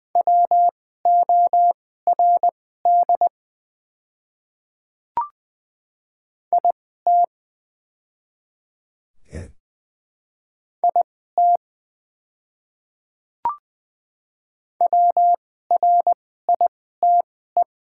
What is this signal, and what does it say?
Morse code, heard as a keyed tone, about 20 wpm, spaced as for 15 wpm, 700 Hz. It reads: WORD IT IT WRITE
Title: 2x Character Spacing - Top 100 Words - 20wpm